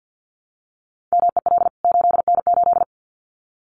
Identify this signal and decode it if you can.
Morse code, heard as a keyed tone, about 50 wpm, 700 Hz. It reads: MEL 8N8
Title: CWT Exchange 50wpm